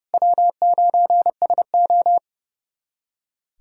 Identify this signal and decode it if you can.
Morse code, heard as a keyed tone, about 32 wpm, 700 Hz. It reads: W9SO
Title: USA Callsign DE Callsign - 4 Characters - 30wpm